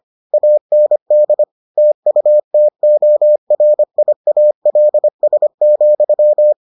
Morse code, heard as a keyed tone, about 25 wpm, 600 Hz. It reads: AND TUTORIALS,